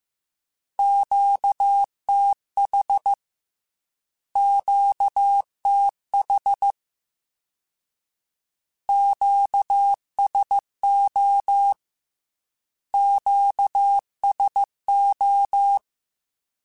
Morse code, heard as a keyed tone, about 15 wpm, 800 Hz. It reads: QTH QTH QSO QSO